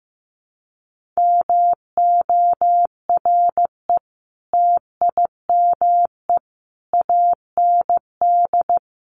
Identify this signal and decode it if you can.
Morse code, heard as a keyed tone, about 15 wpm, 700 Hz. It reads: MORE TIME AND
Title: Sentences from Top 1400 Words 15wpm